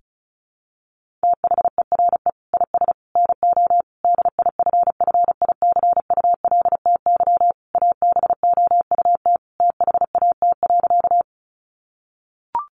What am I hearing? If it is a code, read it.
Morse code, heard as a keyed tone, about 34 wpm, 700 Hz. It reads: THERE IS NO DIFFICULTY ABOUT THAT.
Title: Sentences from Top 1500 Words 35wpm